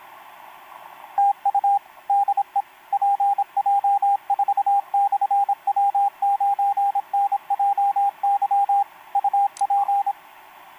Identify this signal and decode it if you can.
Morse code, heard as a keyed tone, about 26 wpm, 800 Hz, with radio noise. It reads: TU DE PJ4/W9NJY UP
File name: pj4-w9njy f5upo etc 12m cw 28mar2011 1924